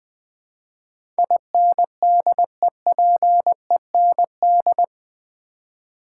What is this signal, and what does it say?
Morse code, heard as a keyed tone, about 20 wpm, 700 Hz. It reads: INDEPEND
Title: Build A Word - 4 Letters At A Time - 12-Letter Words - 20wpm